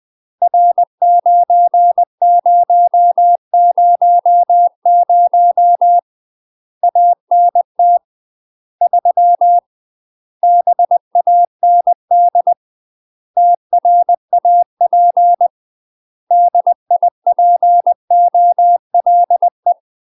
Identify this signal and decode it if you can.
Morse code, heard as a keyed tone, about 20 wpm, 700 Hz. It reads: R9000 ANT 3 BAND TRAP DIPOLE
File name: UK MASTER QSOs TRAINING / 20WPM / UK MASTER QSOs 0004